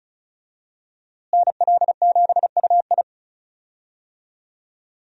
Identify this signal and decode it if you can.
Morse code, heard as a keyed tone, about 34 wpm, 700 Hz. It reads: NL7UI
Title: USA Call Signs - 5 Characters - Repeated 2x - 35wpm